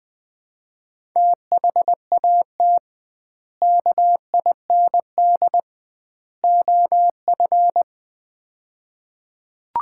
Morse code, heard as a keyed tone, about 20 wpm, 700 Hz. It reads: THAT KIND OF
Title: 2x Word-Spacing - Sets of 3 Words - 20wpm